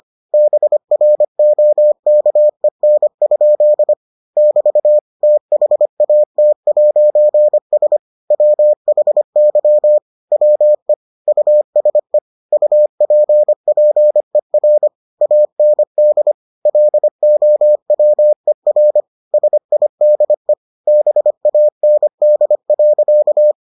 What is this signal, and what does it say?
Morse code, heard as a keyed tone, about 25 wpm, 600 Hz. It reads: BROKEN? <BT> THAT'S WHY WE USE UPPER AND LOWER SIDE BAND.